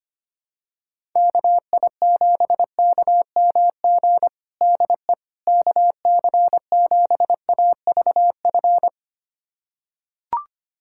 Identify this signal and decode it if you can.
Morse code, heard as a keyed tone, about 25 wpm, 700 Hz. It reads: KI7KMG DE KC7AVF
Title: USA Callsign DE Callsign - 6 Characters - 25wpm